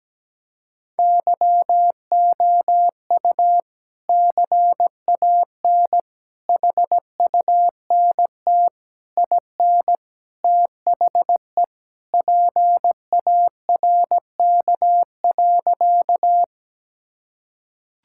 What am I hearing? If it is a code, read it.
Morse code, heard as a keyed tone, about 17 wpm, 700 Hz. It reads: YOU CAN HUNT IN THE PARK.
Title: Sentences from Top 1600 Words 17wpm